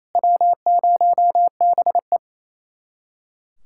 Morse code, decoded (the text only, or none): W0BE